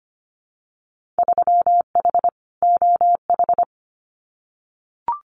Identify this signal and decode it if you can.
Morse code, heard as a keyed tone, about 25 wpm, 700 Hz. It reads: 3H OH
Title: Winter Field Day Exchanges Encore 25wpm